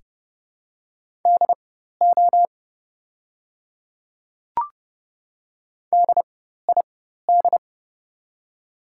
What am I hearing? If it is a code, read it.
Morse code, heard as a keyed tone, about 31 wpm, spaced as for 14 wpm, 700 Hz. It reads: DO DID
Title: 4x Character Spacing - Top 100 Words - 30wpm